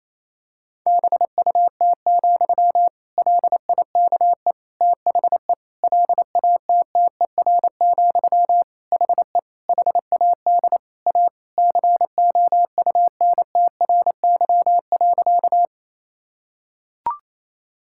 Morse code, heard as a keyed tone, about 27 wpm, 700 Hz. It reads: BUT, LIKE THE LATTER, HE HAD A COUNTRY.
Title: Sentences from Top 1600 Words 28wpm